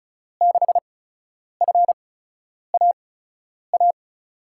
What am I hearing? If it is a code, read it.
Morse code, heard as a keyed tone, about 34 wpm, spaced as for 10 wpm, 700 Hz. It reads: 6FAA
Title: USA Call Signs - 6 Characters - 8x Character Spacing - 35wpm